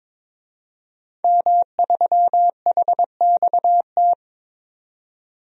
Morse code, heard as a keyed tone, about 22 wpm, 700 Hz. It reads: M3HXT
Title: DXCC-100 Call Signs - 5 Characters - 22wpm